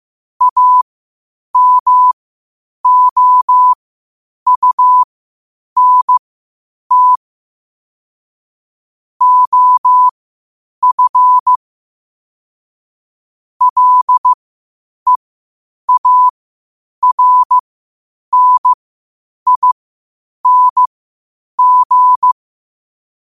Morse code, heard as a keyed tone, about 15 wpm, spaced as for 8 wpm, 1000 Hz. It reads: AMOUNT OF LEARNING
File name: qotdfiles / 8wpm